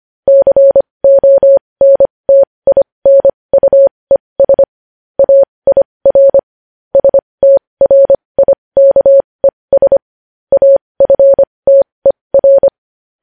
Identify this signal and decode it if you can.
Morse code, heard as a keyed tone, about 25 wpm, spaced as for 20 wpm, 550 Hz. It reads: CONTINUES AIR STRIKES AFTER
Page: News Headlines in Morse Code at 20 WPM